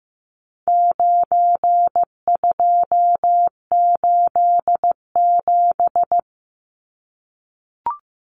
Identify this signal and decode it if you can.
Morse code, heard as a keyed tone, about 15 wpm, 700 Hz. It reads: 9287
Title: Numbers - 4-Digit 15wpm